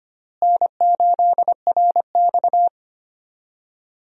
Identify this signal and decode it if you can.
Morse code, heard as a keyed tone, about 25 wpm, 700 Hz. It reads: N8RX
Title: USA Call Signs - 4 Characters - Repeated 3x - 25wpm